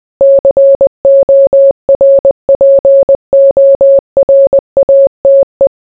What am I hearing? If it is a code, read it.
Morse code, heard as a keyed tone, about 20 wpm, 550 Hz. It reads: CORPORATE